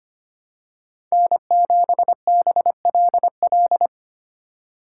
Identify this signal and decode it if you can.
Morse code, heard as a keyed tone, about 25 wpm, 700 Hz. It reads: N7BLL
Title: USA Call Signs - 5 Characters - 2x Character Spacing - 25wpm